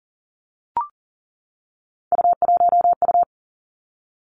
Morse code, heard as a keyed tone, about 41 wpm, 700 Hz. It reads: U1U
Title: Three Challenging Letter-Number Combinations 40wpm